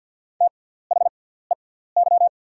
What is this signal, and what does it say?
Morse code, heard as a keyed tone, about 50 wpm, spaced as for 17 wpm, 700 Hz. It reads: THEY
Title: 6x Character Spacing - Top 100 Words - 50wpm